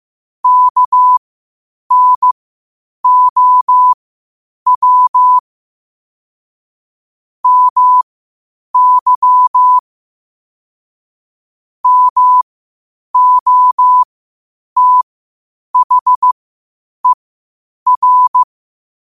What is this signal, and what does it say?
Morse code, heard as a keyed tone, about 15 wpm, spaced as for 8 wpm, 1000 Hz. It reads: KNOW MY MOTHER
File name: qotdfiles / 8wpm